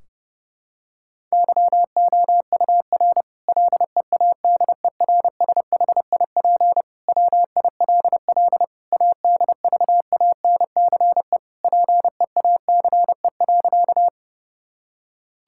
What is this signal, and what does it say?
Morse code, heard as a keyed tone, about 32 wpm, 700 Hz. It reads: YOUR LEADERSHIP WILL ADVANCE PEACE.
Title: Sentences from Top 1600 Words 30wpm